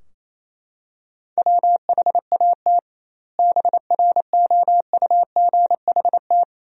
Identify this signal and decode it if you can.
Morse code, heard as a keyed tone, about 28 wpm, 700 Hz. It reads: WHAT BROUGHT
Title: 2x Word-Spacing - Sentences from Top 500 Words - 28wpm